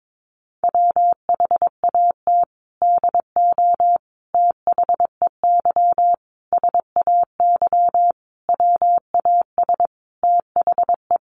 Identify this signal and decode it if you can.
Morse code, heard as a keyed tone, about 22 wpm, 700 Hz. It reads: WHAT DO THEY SAY WAS THE
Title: Sentences from Top 200 Words 22wpm